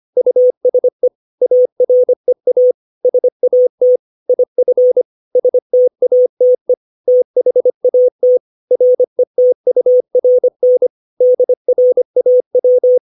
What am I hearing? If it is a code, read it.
Morse code, heard as a keyed tone, about 25 wpm, 500 Hz. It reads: USE AREA SAT IF STATE THAT RETURN DRAW
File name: 632 words / 632 Words 25WPM